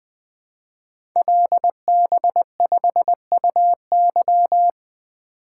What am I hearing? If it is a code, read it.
Morse code, heard as a keyed tone, about 20 wpm, 700 Hz. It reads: LB5UY
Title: Code Groups - 5 Letter:Numbers - 20wpm